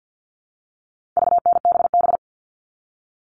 Morse code, heard as a keyed tone, about 50 wpm, 700 Hz. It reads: VNBB
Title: All Cut Numbers - 4-Digit 50wpm